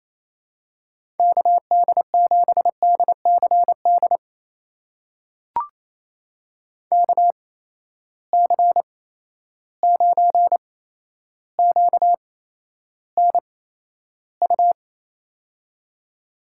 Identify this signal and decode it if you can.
Morse code, heard as a keyed tone, about 28 wpm, spaced as for 12 wpm, 700 Hz. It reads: KD7DCD KC9QNU
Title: USA Call Signs - 6 Characters - 8x Character Spacing - 28wpm